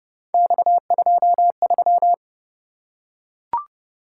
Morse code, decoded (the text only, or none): X23